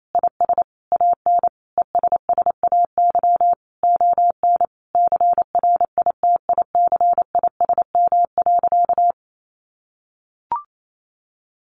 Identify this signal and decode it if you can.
Morse code, heard as a keyed tone, about 28 wpm, 700 Hz. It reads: IS AN ESSAY ON CRITICISM.
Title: Sentences from Top 1500 Words 28wpm